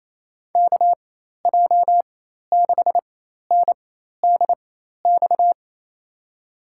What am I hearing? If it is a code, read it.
Morse code, heard as a keyed tone, about 28 wpm, spaced as for 13 wpm, 700 Hz. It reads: KJ6NDX
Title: USA Call Signs - 4x Character Spacing - 28wpm